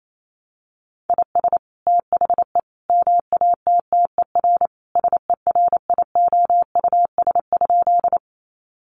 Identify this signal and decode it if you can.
Morse code, heard as a keyed tone, about 28 wpm, 700 Hz. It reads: IS THE MATTER SERIOUS?
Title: Sentences from Top 1100 Words 28wpm